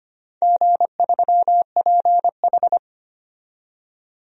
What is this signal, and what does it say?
Morse code, heard as a keyed tone, about 25 wpm, 700 Hz. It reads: G3PH
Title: DXCC-100 Call Signs - 4 Characters - 25wpm